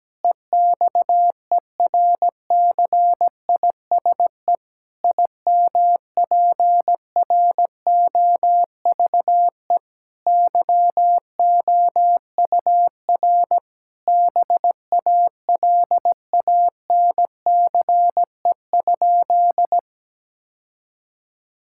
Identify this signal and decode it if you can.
Morse code, heard as a keyed tone, about 17 wpm, 700 Hz. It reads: EXERCISE IMPROVE YOUR BALANCE?
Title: Sentences from Top 1000 Words - Review Part 6 - 17wpm